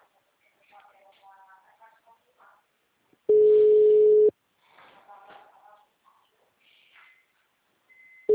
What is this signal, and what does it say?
Ringback tone (Europe, 425 Hz), with speech.
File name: out- -2008- - - .0